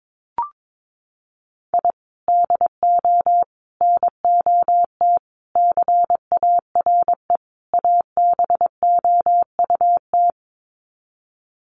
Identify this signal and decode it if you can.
Morse code, heard as a keyed tone, about 22 wpm, 700 Hz. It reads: I DO NOT CARE ABOUT
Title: Sets of 5 Words Encore2 22wpm